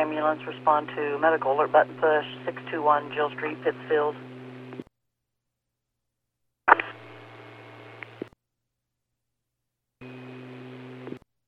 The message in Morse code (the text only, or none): none